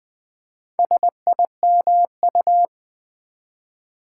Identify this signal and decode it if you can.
Morse code, heard as a keyed tone, about 20 wpm, 700 Hz. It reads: SIMU
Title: Build A Word - 4 Letters At A Time - 10-Letter Words - 20wpm